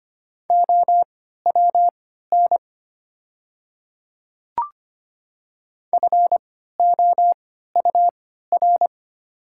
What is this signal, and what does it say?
Morse code, heard as a keyed tone, about 25 wpm, spaced as for 14 wpm, 690 Hz. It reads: OWN FOUR
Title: 3x Character Spacing - Top 500 Words - 25wpm